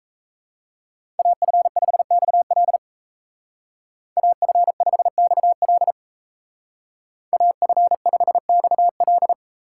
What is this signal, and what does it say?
Morse code, heard as a keyed tone, about 36 wpm, 700 Hz. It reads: AF5XL AF5XL AF5XL